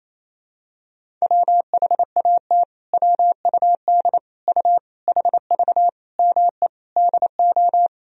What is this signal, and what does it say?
Morse code, heard as a keyed tone, about 28 wpm, 700 Hz. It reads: WHAT WUD U HV ME DO
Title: Sentences from Top 300 Words and Abbreviations 28wpm